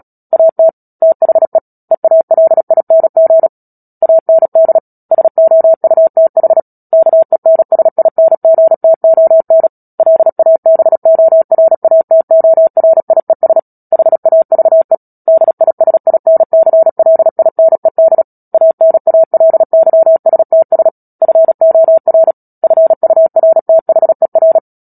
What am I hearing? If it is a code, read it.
Morse code, heard as a keyed tone, about 37 wpm, 650 Hz. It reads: AT THE EALING AND SOUTH KENSINGTON LABORATORIES HAVE DISINCLINED ANALYSTS FOR FURTHER
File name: Book Two / 10-Chapter 10 - Epilogue - 36 WPM